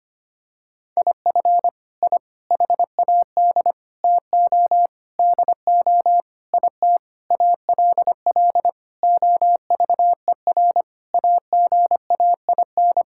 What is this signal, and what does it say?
Morse code, heard as a keyed tone, about 25 wpm, 700 Hz. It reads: IF I HAD TO DO IT ALL OVER AGAIN
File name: practice / 25wpm / Sentences with 9-Word Phrases - 25wpm